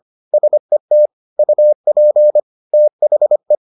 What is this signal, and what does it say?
Morse code, heard as a keyed tone, about 25 wpm, 600 Hz. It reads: SET UP THE